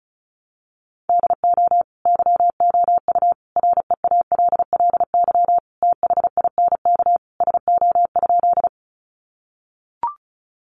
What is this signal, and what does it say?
Morse code, heard as a keyed tone, about 34 wpm, 700 Hz. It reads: DO YOU REALLY THINK SO?
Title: Sentences from Top 600 Words 35wpm